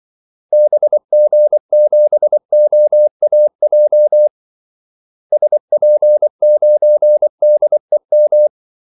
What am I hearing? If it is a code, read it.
Morse code, heard as a keyed tone, about 24 wpm, 600 Hz. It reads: BG7OAJ SP9DEM
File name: Callsigns-24wpm